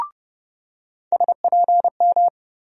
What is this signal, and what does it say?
Morse code, heard as a keyed tone, about 30 wpm, 700 Hz. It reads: SPM
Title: USA Call Sign Postfixes - 3 Letters 30wpm